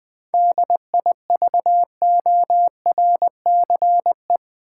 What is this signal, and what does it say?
Morse code, heard as a keyed tone, about 20 wpm, 700 Hz. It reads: DIVORCE
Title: 7 Letter Words - First 3 Letters Given - 20wpm